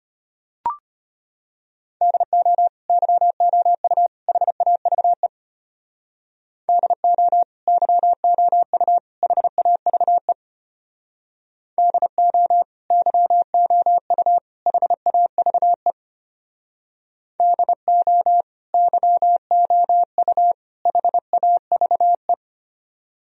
Morse code, heard as a keyed tone, about 32 wpm, 700 Hz. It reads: DO YOU HAVE DO YOU HAVE DO YOU HAVE DO YOU HAVE